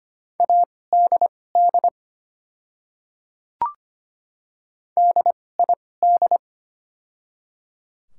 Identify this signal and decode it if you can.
Morse code, heard as a keyed tone, about 25 wpm, spaced as for 18 wpm, 690 Hz. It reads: ADD DID
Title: 2x Character Spacing - Top 200 Words - 25wpm